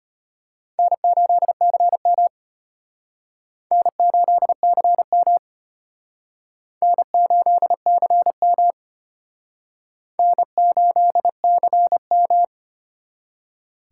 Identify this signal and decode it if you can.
Morse code, heard as a keyed tone, about 31 wpm, 700 Hz. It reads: N8CM N8CM N8CM N8CM
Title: Speed Racing - USA 1x2 Call Signs 25wpm